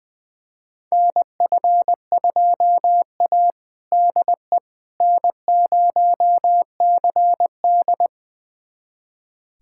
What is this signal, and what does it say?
Morse code, heard as a keyed tone, about 20 wpm, 700 Hz. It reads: NF2A DE N0CD
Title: USA Callsign DE Callsign - 4 Characters - 20wpm